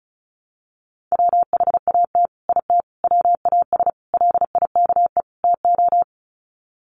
Morse code, heard as a keyed tone, about 34 wpm, 700 Hz. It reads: WHAT IT WAS LIKE TO
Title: Sets of 5 Words Encore 35wpm